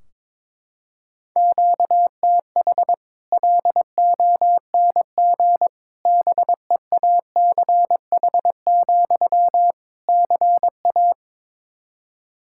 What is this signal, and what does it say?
Morse code, heard as a keyed tone, about 22 wpm, 700 Hz. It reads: QTH LONG BEACH, CA
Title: Practice QSO - Exchange 1 - Beginning - 22wpm